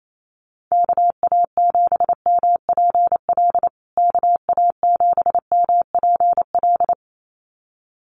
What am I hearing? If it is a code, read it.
Morse code, heard as a keyed tone, about 28 wpm, 700 Hz. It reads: KA7MPL KA7MPL
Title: USA Call Signs Repeated 2x 28wpm